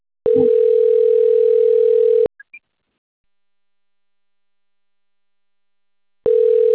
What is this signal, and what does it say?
Ringback tone (North America), 440 + 480 Hz.